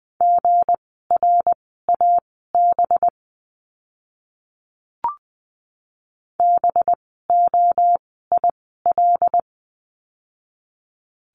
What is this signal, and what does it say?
Morse code, heard as a keyed tone, about 20 wpm, spaced as for 14 wpm, 700 Hz. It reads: GRAB BOIL
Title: 185 Lesson 30f - Letter b - Words 20wpm